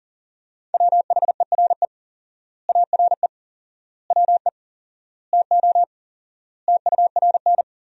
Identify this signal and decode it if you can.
Morse code, heard as a keyed tone, about 40 wpm, 700 Hz. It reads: WHERE ARE WE TO TURN
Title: 4x Word-Spacing - Sentences from Top 200 Words - 40wpm